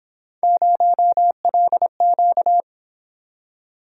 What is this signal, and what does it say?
Morse code, heard as a keyed tone, about 29 wpm, 700 Hz. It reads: 0LQ